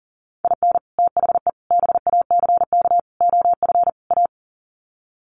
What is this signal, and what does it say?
Morse code, heard as a keyed tone, about 40 wpm, 700 Hz. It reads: IN THE BACK OF A
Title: Sets of 5 Words Encore2 40wpm